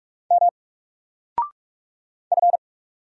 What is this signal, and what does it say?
Morse code, heard as a keyed tone, about 44 wpm, 700 Hz. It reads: M F
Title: Single Letter-Number Alt 45wpm